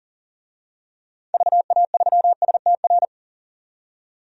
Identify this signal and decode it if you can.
Morse code, heard as a keyed tone, about 40 wpm, 700 Hz. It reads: VA3STR
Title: DXCC-100 Call Signs - 6 Characters - 40wpm